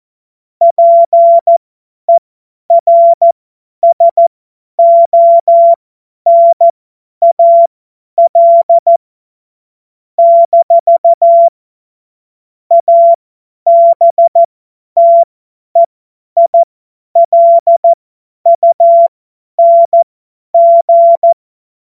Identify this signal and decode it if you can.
Morse code, heard as a keyed tone, about 14 wpm, spaced as for 9 wpm, 680 Hz. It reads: PERSONAL-ABTEILUNG
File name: Witz-08-Juli-2025-50-F70